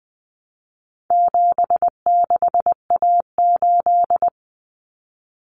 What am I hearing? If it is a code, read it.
Morse code, heard as a keyed tone, about 20 wpm, 700 Hz. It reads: 76A8
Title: Common Cut Numbers - 4-Digit 20wpm